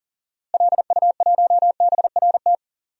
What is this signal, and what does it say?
Morse code, heard as a keyed tone, about 40 wpm, 700 Hz. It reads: LU1BRT
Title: DXCC-100 Call Signs - 6 Characters - Repeated 3x - 40wpm